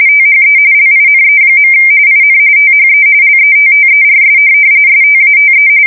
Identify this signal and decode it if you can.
RTTY, about 45.45 baud. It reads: THE LINE. IN THE LETTERS STATE
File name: fsk170 / teletype-RTTY